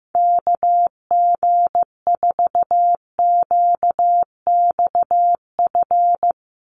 Morse code, heard as a keyed tone, about 15 wpm, 700 Hz. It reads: KG4QXF
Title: USA Call Signs 15wpm